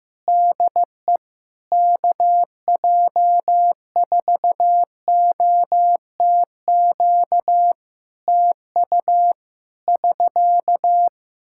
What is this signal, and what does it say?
Morse code, heard as a keyed tone, about 15 wpm, 700 Hz. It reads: DE KJ4OTQ TU <SK>
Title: Practice QSO - Exchange 4 - Ending - 15wpm